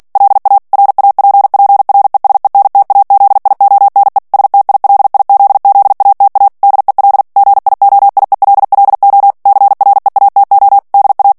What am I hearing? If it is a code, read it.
Morse code, heard as a keyed tone, about 48 wpm, 800 Hz. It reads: LA RAPPRESENTAZIONE STILIZZATA DEL GIOIELLO CREATO DA
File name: morse mp3 / ITALIEN texte clair 1 vitesse 45 mots minute